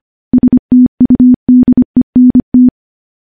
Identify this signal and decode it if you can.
Morse code, heard as a keyed tone, about 25 wpm, 250 Hz. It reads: STUDENT
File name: morse / morse-training-25WPM